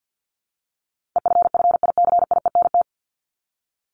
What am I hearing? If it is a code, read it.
Morse code, heard as a keyed tone, about 50 wpm, 700 Hz. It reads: EFFICIENT
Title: Top 1251-1500 Words 50wpm